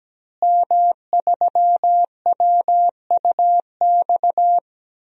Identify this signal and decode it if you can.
Morse code, heard as a keyed tone, about 17 wpm, 700 Hz. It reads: M3WUX
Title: DXCC-100 Call Signs - 5 Characters - Repeated 3x - 17wpm